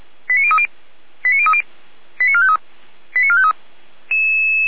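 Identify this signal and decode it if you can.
Five-tone selective call (ZVEI), heard as quick runs of five tones, tones of about 70 ms. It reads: 89030 89030 89453 89453
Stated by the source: received on SDR, ZVEI five-tone selective call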